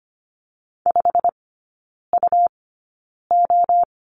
Morse code, heard as a keyed tone, about 25 wpm, spaced as for 9 wpm, 700 Hz. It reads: 5UO
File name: Lesson 17e - Letter u - Three Characters 25wpm